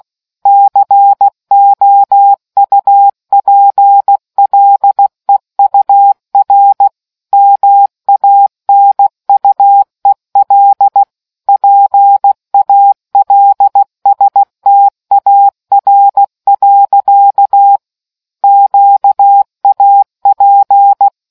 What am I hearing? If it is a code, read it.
Morse code, heard as a keyed tone, about 16 wpm, 800 Hz. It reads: COUPLEUR MANUEL PALSTAR. QAP